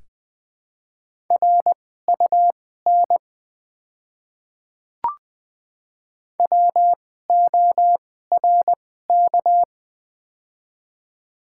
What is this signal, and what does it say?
Morse code, heard as a keyed tone, about 20 wpm, spaced as for 14 wpm, 700 Hz. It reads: RUN WORK
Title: 2x Character Spacing - Top 300 Words - 20wpm